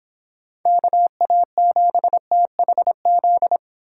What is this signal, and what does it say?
Morse code, heard as a keyed tone, about 26 wpm, 700 Hz. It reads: KA7THZ